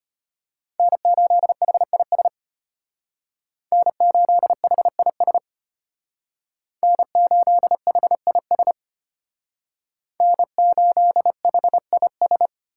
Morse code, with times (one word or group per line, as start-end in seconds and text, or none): 0.8-2.3 N8HIS
3.7-5.4 N8HIS
6.8-8.7 N8HIS
10.2-12.5 N8HIS